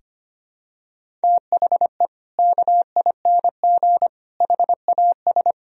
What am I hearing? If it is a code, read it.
Morse code, heard as a keyed tone, about 25 wpm, 700 Hz. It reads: THE KING HAS
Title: Sentences from Top 1000 Words - Review Part 2 - 25wpm